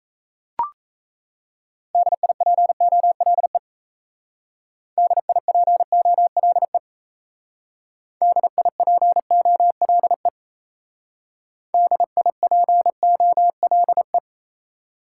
Morse code, heard as a keyed tone, about 36 wpm, 700 Hz. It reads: DIPOLE DIPOLE DIPOLE DIPOLE